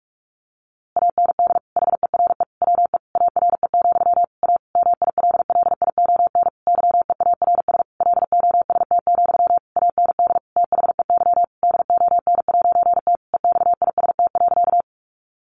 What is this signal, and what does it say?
Morse code, heard as a keyed tone, about 44 wpm, 700 Hz. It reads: AND HERE WE ARE, A MILLION YEARS LOST, AND THEY DON'T EXIST.
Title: A Sound Of Thunder 45wpm